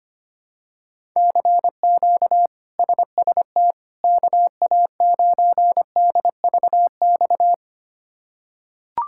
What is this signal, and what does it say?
Morse code, heard as a keyed tone, about 25 wpm, 700 Hz. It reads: CQ SST KA9DVX
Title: K1USN SST QSO Practice - 25wpm